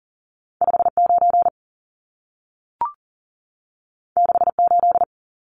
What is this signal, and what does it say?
Morse code, heard as a keyed tone, about 40 wpm, 700 Hz. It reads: 59 68
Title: Numbers - 2-Digit 40wpm